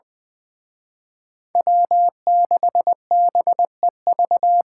Morse code, heard as a keyed tone, about 24 wpm, 700 Hz. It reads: W6BEV
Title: Speed Racing - USA 1x3 Call Signs 20wpm